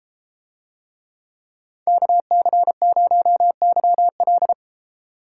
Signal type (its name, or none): Morse code